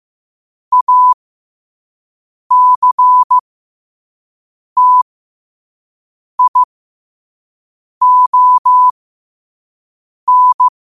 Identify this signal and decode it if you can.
Morse code, heard as a keyed tone, about 15 wpm, spaced as for 5 wpm, 1000 Hz. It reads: ACTION
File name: qotdfiles / 5wpm